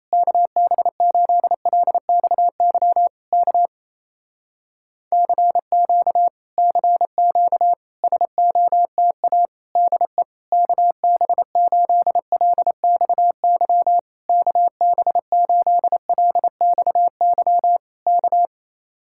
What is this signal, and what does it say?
Morse code, heard as a keyed tone, about 28 wpm, 700 Hz. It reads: KB8LXY K CQ CQ SOTA DE KB8LXY KB8LXY K
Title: Speed-Racing Common QSO Phrases Encore - 28wpm